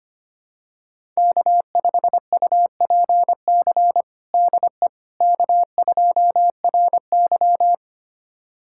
Morse code, heard as a keyed tone, about 25 wpm, 700 Hz. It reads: K5UPC DE K2RY
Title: USA Callsign DE Callsign 25wpm